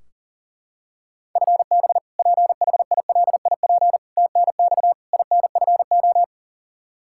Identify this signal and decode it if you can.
Morse code, heard as a keyed tone, about 40 wpm, 700 Hz. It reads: FB PHILIP TNX INFO
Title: Practice QSO - Exchange 3 - More Info - 40wpm